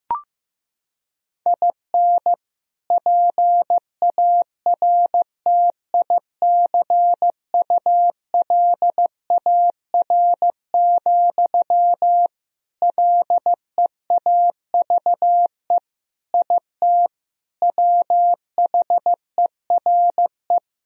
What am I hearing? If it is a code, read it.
Morse code, heard as a keyed tone, about 16 wpm, 700 Hz. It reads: IN PARTICULAR, LEAVE IT WHERE
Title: Sentences from Top 1000 Words - Review Part 4 - 15wpm